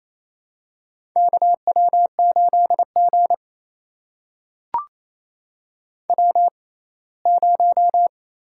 Morse code, heard as a keyed tone, about 28 wpm, spaced as for 19 wpm, 700 Hz. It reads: KW8G W0
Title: USA Call Signs - 4 Characters - 6x Character Spacing - 28wpm